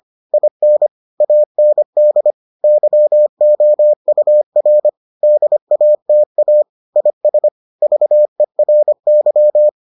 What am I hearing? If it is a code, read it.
Morse code, heard as a keyed tone, about 25 wpm, 600 Hz. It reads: IN AND YOUR DATA IS VERY